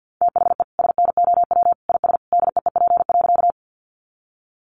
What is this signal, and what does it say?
Morse code, heard as a keyed tone, about 50 wpm, 700 Hz. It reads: THE SNOW IS DEEP.
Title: Sentences from Top 500 Words 50wpm